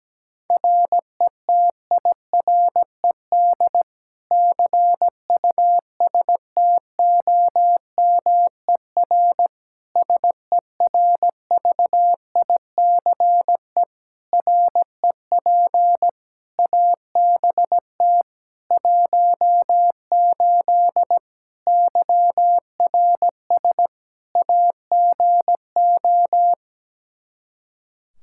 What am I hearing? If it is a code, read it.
Morse code, heard as a keyed tone, about 17 wpm, 700 Hz. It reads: RETIRED CUSTOMER SERVICE REP ABT 18 YRS AGO
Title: Practice QSO - Exchange 3 - More Info - 17wpm